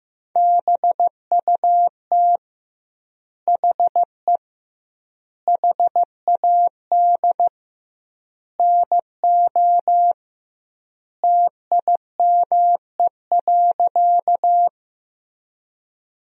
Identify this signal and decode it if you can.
Morse code, heard as a keyed tone, about 15 wpm, 700 Hz. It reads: BUT HE HAD NO TIME.
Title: 2x Word-Spacing - Sentences from Top 100 Words - 15wpm